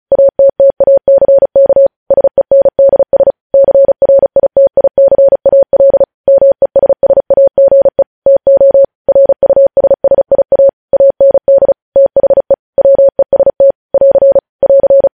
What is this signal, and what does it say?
Morse code, heard as a keyed tone, about 36 wpm, 550 Hz. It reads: ATTACK SENDS CRITICAL MESSAGE TO RUSSIA AND THE WEST <AR> <AR>
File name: podcast / News headlines-1x-35x35